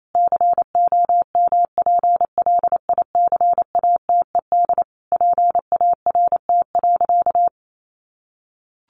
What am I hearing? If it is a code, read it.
Morse code, heard as a keyed tone, about 28 wpm, 700 Hz. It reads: COMPLICATED PART.
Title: Sentences from Top 1200 Words 28wpm